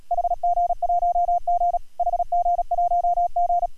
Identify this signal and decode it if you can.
Morse code, heard as a keyed tone, about 34 wpm, 690 Hz, with radio noise. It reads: HG1G HG1G